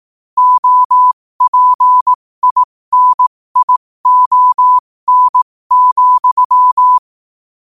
Morse code, heard as a keyed tone, about 18 wpm, spaced as for 15 wpm, 1000 Hz. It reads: OPINION,
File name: qotdfiles / 15wpm